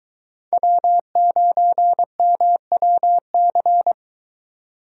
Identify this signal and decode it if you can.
Morse code, heard as a keyed tone, about 23 wpm, 700 Hz. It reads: W9MWC